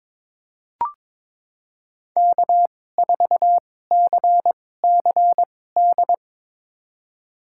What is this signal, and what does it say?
Morse code, heard as a keyed tone, about 22 wpm, spaced as for 16 wpm, 700 Hz. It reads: K4CCD
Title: USA Call Signs - 5 Characters - 2x Character Spacing - 22wpm